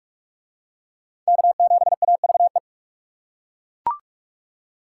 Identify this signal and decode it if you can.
Morse code, heard as a keyed tone, about 41 wpm, 700 Hz. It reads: K7AVE